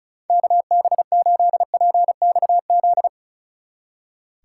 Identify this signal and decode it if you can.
Morse code, heard as a keyed tone, about 43 wpm, 700 Hz. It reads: KB8PXZ
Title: Speed Racing - USA 2x3 Call Signs 35wpm